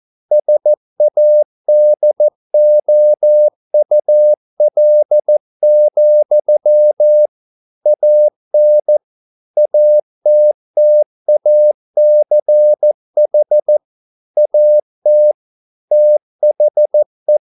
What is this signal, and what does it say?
Morse code, heard as a keyed tone, about 14 wpm, 600 Hz. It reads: SADOUL, AN ATTACH AT THE